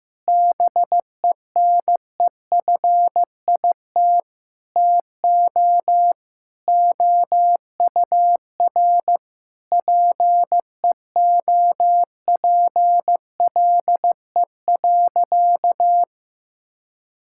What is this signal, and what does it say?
Morse code, heard as a keyed tone, about 16 wpm, 700 Hz. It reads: BENEFIT TO OUR PEOPLE.
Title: Sentences from Top 1000 Words - Review Part 6 - 15wpm